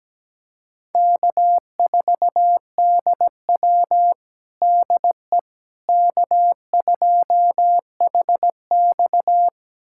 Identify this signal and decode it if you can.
Morse code, heard as a keyed tone, about 17 wpm, 700 Hz. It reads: K4DW DE K2HX
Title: USA Callsign DE Callsign - 4 Characters - 17wpm